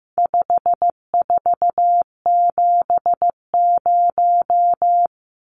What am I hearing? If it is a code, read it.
Morse code, heard as a keyed tone, about 15 wpm, 700 Hz. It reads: 5470
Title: Numbers - 4-Digit 15wpm